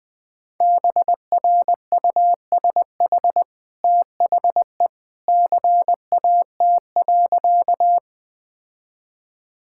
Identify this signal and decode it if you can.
Morse code, heard as a keyed tone, about 20 wpm, 700 Hz. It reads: BRUSH THE CAT.
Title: Easy Sentences with Hints - 20wpm